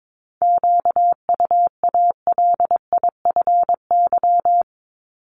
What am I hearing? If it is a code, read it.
Morse code, heard as a keyed tone, about 22 wpm, 700 Hz. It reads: QUALIFY
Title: Top 1751-2000 Words 22wpm